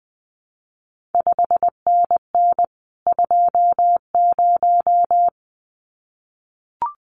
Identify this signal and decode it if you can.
Morse code, heard as a keyed tone, about 20 wpm, 700 Hz. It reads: 5NN 20
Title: CQ WW DX Contest Exchange Encore - 20wpm